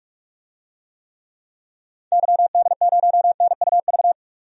Morse code, heard as a keyed tone, about 44 wpm, 700 Hz. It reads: YD0NUV
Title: DXCC-100 Call Signs - 6 Characters - 45wpm